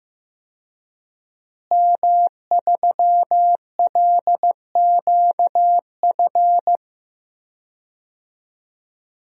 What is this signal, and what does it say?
Morse code, heard as a keyed tone, about 15 wpm, 700 Hz. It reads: M3LQF
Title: DXCC-100 Call Signs - 5 Characters - Repeated 2x - 15wpm